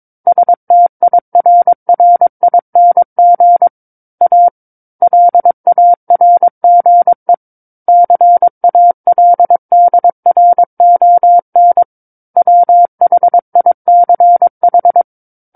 Morse code, heard as a keyed tone, about 22 wpm, 700 Hz. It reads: STIRRING A LARGE CALDRON WHICH
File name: 06-Chapter-06 22 WPM